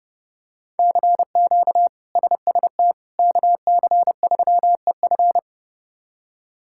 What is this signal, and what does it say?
Morse code, heard as a keyed tone, about 32 wpm, 700 Hz. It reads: CQ SST KC3EF
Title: K1USN SST QSO Practice - 30wpm